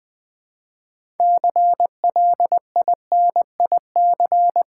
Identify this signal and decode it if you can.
Morse code, heard as a keyed tone, about 20 wpm, 700 Hz. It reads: CLINIC
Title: Build A Word - 2 Letters At A Time - 6-Letter Words - 20wpm